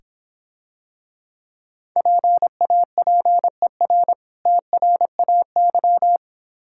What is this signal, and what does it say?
Morse code, heard as a keyed tone, about 26 wpm, 700 Hz. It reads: PAPER TRAY